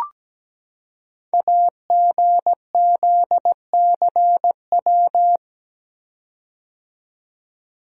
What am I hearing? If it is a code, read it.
Morse code, heard as a keyed tone, about 17 wpm, 700 Hz. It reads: AGZCW
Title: Code Groups - 5 Letters - 17wpm